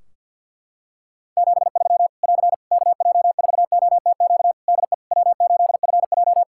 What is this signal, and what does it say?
Morse code, heard as a keyed tone, about 53 wpm, 700 Hz. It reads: 73 <AR> KJ4OTQ DE W8FJ
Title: Practice QSO - Exchange 4 - Ending - 50wpm